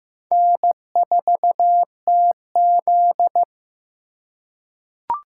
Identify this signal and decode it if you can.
Morse code, heard as a keyed tone, about 15 wpm, 700 Hz. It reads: N4TZ
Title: K1USN SST QSO Practice - 15wpm